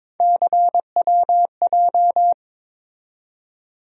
Morse code, heard as a keyed tone, about 21 wpm, 700 Hz. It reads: CWJ
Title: USA Call Sign Postfixes - 3 Letters 22wpm